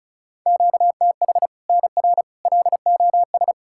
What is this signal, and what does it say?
Morse code, heard as a keyed tone, about 36 wpm, 700 Hz. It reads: QTH NR LOS
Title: Practice QSO - Exchange 1 - Beginning - 35wpm